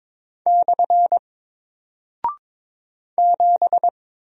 Morse code, heard as a keyed tone, about 22 wpm, 700 Hz. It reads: / 7
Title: Single Character (Letter:Number:Punctuation:Prosign) Alt 22wpm